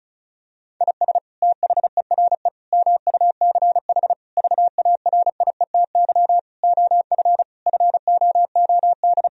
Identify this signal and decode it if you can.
Morse code, heard as a keyed tone, about 34 wpm, 700 Hz. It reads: IS THERE MUCH VARIETY OF FOOD
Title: Sentences from Top 1000 Words - Review Part 4 - 35wpm